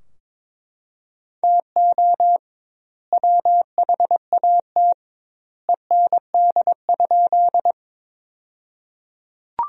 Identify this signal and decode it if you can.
Morse code, heard as a keyed tone, about 22 wpm, 700 Hz. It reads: TO WHAT END?
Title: 2x Word-Spacing - Sentences from Top 200 Words - 22wpm